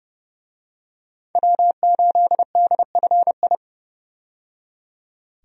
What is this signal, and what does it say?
Morse code, heard as a keyed tone, about 29 wpm, 700 Hz. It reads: W8DFI
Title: USA Call Signs - 5 Characters - 30wpm